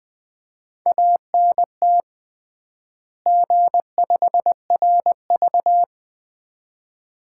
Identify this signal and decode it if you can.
Morse code, heard as a keyed tone, about 20 wpm, 700 Hz. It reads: ANT G5RV
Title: Common QSO Phrases - 3x Word-Spacing - 20wpm